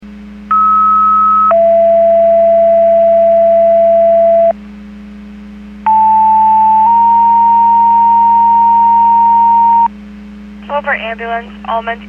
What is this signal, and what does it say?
Two-tone page, heard as a one-second tone, then a three-second tone, about 1290 then 670 Hz, about 880 then 930 Hz, with speech. It is heard from 0.5 to 9.9 s.